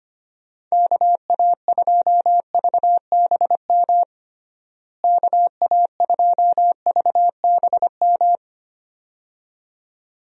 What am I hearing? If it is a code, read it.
Morse code, heard as a keyed tone, about 25 wpm, 700 Hz. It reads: KA2VBM KA2VBM
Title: USA Call Signs - 6 Characters - Repeated 2x - 25wpm